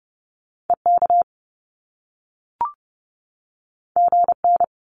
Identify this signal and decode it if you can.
Morse code, heard as a keyed tone, about 30 wpm, 700 Hz. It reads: EK GN
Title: Top Two Letter Combinations in Words 30wpm